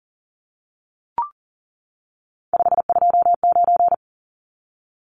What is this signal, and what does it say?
Morse code, heard as a keyed tone, about 40 wpm, 700 Hz. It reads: 529
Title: Numbers - 3-Digit 40wpm